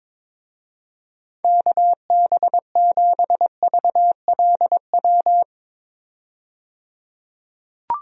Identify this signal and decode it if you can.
Morse code, heard as a keyed tone, about 22 wpm, 700 Hz. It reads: KB7VLW
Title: USA Call Signs - 6 Characters - 22wpm